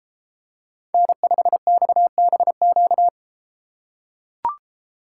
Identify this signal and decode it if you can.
Morse code, heard as a keyed tone, about 33 wpm, 700 Hz. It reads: N5XBQ